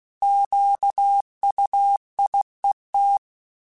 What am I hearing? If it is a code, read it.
Morse code, heard as a keyed tone, about 16 wpm, 800 Hz. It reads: QUIET